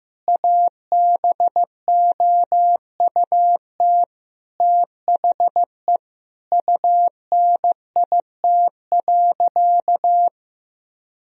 Morse code, heard as a keyed tone, about 15 wpm, 700 Hz. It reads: ABOUT THE UNIT.
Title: Sentences from Top 1000 Words - Review Part 2 - 15wpm